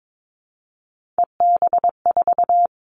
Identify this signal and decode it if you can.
Morse code, heard as a keyed tone, about 22 wpm, 700 Hz. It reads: EB4
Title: All Cut Numbers - 3-Digit 22wpm